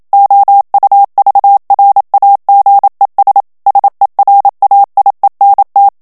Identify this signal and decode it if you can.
Morse code, heard as a keyed tone, about 27 wpm, 800 Hz. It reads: OUVRAGES SERAIENT
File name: FRANCAIS TEXTE CLAIR 2 VITESSE 26 MOTS MINUTES